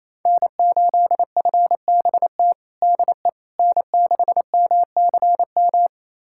Morse code, heard as a keyed tone, about 28 wpm, 700 Hz. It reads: N8FBT DE N6MCM
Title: USA Callsign DE Callsign - 5 Characters - 28wpm